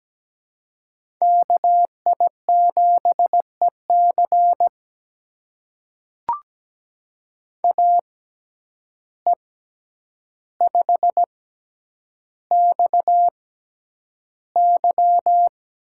Morse code, heard as a keyed tone, about 17 wpm, spaced as for 9 wpm, 700 Hz. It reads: KI7EC AE5XY
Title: USA Call Signs - 5 Characters - 6x Character Spacing - 17wpm